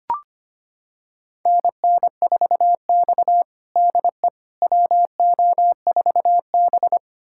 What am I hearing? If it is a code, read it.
Morse code, heard as a keyed tone, about 25 wpm, 700 Hz. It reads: NN4X DE WO4B
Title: USA Callsign DE Callsign - 4 Characters - 25wpm